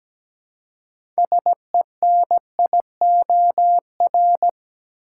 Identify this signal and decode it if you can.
Morse code, heard as a keyed tone, about 17 wpm, 700 Hz. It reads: SENIOR
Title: Six Letter Words - 17wpm